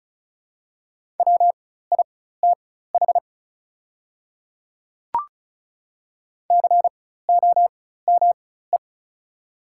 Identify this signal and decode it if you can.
Morse code, heard as a keyed tone, about 36 wpm, spaced as for 16 wpm, 700 Hz. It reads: WITH COME
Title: 4x Character Spacing - Top 100 Words - 35wpm